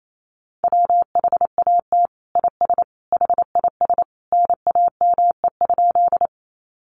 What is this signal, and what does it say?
Morse code, heard as a keyed tone, about 28 wpm, 700 Hz. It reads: WHAT IS HIS NAME?
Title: Sentences from Top 200 Words 28wpm